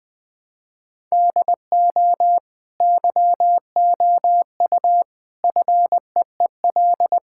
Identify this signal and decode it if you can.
Morse code, heard as a keyed tone, about 20 wpm, 700 Hz. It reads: DO YOU FEEL
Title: Sentences from Top 1000 Words - Review Part 5 - 20wpm